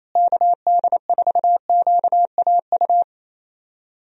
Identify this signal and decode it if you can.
Morse code, heard as a keyed tone, about 28 wpm, 700 Hz. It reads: KD4QAU
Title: USA Call Signs - 6 Characters - Repeated 3x - 28wpm